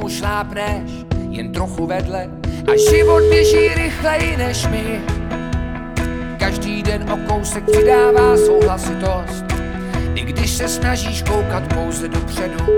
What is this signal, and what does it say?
Ringback tone (Europe, 425 Hz), with speech.